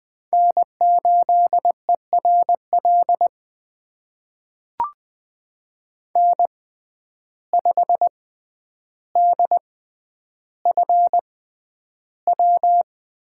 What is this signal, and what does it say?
Morse code, heard as a keyed tone, about 20 wpm, spaced as for 10 wpm, 700 Hz. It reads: N8ERL N5DFW
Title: USA Call Signs - 5 Characters - 6x Character Spacing - 20wpm